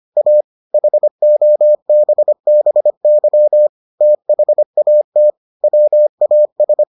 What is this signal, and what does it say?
Morse code, heard as a keyed tone, about 25 wpm, 600 Hz. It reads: A HOBBY THAT WAS